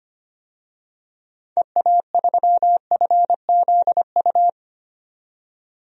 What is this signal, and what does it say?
Morse code, heard as a keyed tone, about 25 wpm, 700 Hz. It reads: EA3FZU
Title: DXCC-100 Call Signs - 6 Characters - 25wpm